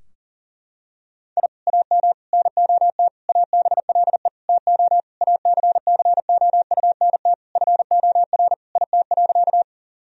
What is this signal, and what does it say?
Morse code, heard as a keyed tone, about 40 wpm, 700 Hz. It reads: I AM NOT ABLE TO ACCOUNT FOR IT.
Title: Sentences from Top 1000 Words - Review Part 2 - 40wpm